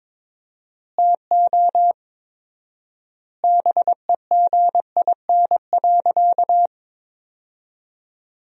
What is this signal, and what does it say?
Morse code, heard as a keyed tone, about 22 wpm, 700 Hz. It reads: TO BEGIN.
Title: 4x Word-Spacing - Sentences from Top 300 Words - 22wpm